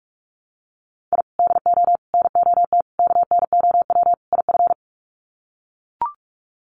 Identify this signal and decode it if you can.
Morse code, heard as a keyed tone, about 44 wpm, 700 Hz. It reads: I DO NOT KNOW IF
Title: Sets of 5 Words 45wpm